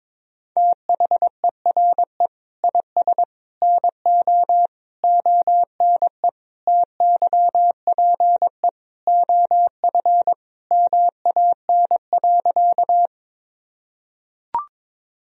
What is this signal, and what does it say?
Morse code, heard as a keyed tone, about 21 wpm, 700 Hz. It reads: THERE IS NO ONE TYPE OF MAN.
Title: Sentences from Top 1000 Words - Review Part 3 - 22wpm